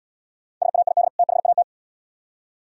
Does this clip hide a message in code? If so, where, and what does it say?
Yes: Morse code from 0.6 to 1.6 s, heard as a keyed tone, about 78 wpm, 700 Hz. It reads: SAID THAT